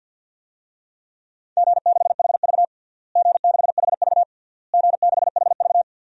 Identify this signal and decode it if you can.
Morse code, heard as a keyed tone, about 50 wpm, 700 Hz. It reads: G6HV G6HV G6HV
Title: DXCC-100 Call Signs - 4 Characters - Repeated 3x - 50wpm